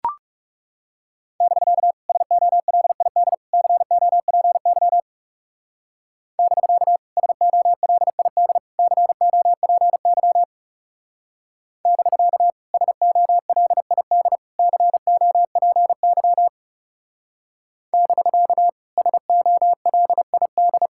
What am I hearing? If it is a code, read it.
Morse code, heard as a keyed tone, about 36 wpm, 700 Hz. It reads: <BK> SOLID COPY <BK> SOLID COPY <BK> SOLID COPY <BK> SOLID
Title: Speed-Racing Common QSO Phrases Encore - 30wpm